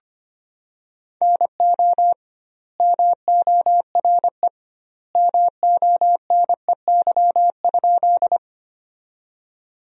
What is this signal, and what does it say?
Morse code, heard as a keyed tone, about 25 wpm, 700 Hz. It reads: NO MORE MONEY?
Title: 2x Word-Spacing - Sentences from Top 400 Words - 25wpm